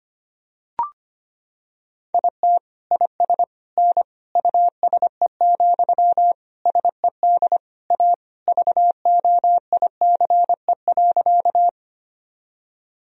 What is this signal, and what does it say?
Morse code, heard as a keyed tone, about 25 wpm, 700 Hz. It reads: IT IS N USE, SED A VOICE.
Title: Sentences from Top 500 Words and Abbreviations 25wpm